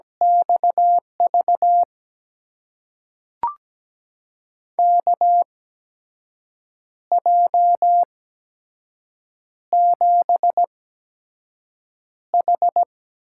Morse code, heard as a keyed tone, about 17 wpm, spaced as for 6 wpm, 700 Hz. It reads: XV KJ7H
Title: USA Call Signs - 8x Character Spacing - 17wpm